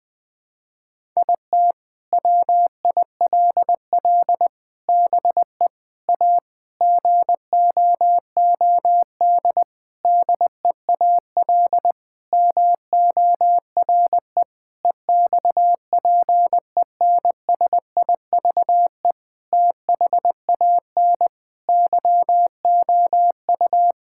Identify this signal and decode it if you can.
Morse code, heard as a keyed tone, about 21 wpm, 700 Hz. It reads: IT WILL BE A GOOD DEAL MORE EXPENSIVE THAN YOU
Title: Sentences from Top 1000 Words - Review Part 6 - 20wpm